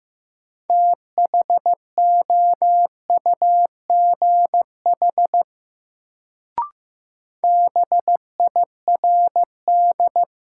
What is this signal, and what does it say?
Morse code, heard as a keyed tone, about 15 wpm, 700 Hz. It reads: THOUGH BIRD
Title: Sentences from Top 400 Words - New Word Drill - 15wpm